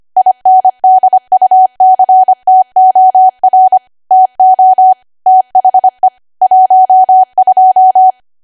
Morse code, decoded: INDUCTOR TO THE 12